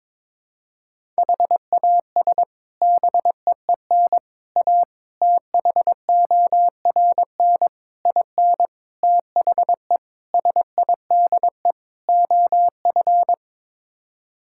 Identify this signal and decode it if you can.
Morse code, heard as a keyed tone, about 22 wpm, 700 Hz. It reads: HAS BEEN A THORN IN THE SIDE OF
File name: practice / 22wpm / Sets of 8 Words - 22wpm